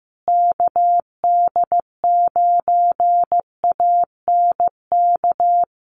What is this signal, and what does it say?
Morse code, heard as a keyed tone, about 15 wpm, 700 Hz. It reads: KD9ANK
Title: USA Call Signs Repeated 3x 15wpm